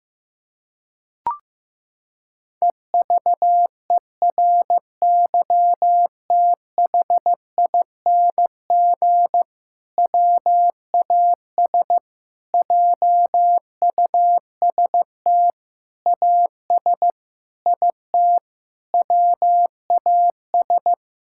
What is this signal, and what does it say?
Morse code, heard as a keyed tone, about 16 wpm, 700 Hz. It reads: EVERYTHING WAS JUST AS IT WAS
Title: Sentences from Top 1600 Words 15wpm